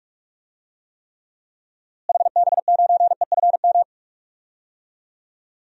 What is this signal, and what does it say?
Morse code, heard as a keyed tone, about 43 wpm, 700 Hz. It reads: HB9EFM
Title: DXCC-100 Call Signs - 6 Characters - Repeated 2x - 45wpm